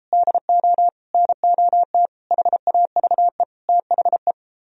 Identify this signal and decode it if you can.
Morse code, heard as a keyed tone, about 33 wpm, 700 Hz. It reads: DO NOT HAVE THE